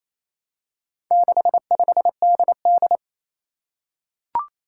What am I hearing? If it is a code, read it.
Morse code, heard as a keyed tone, about 27 wpm, 700 Hz. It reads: 65DD
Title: Four Challenging Letter-Number Combinations 28wpm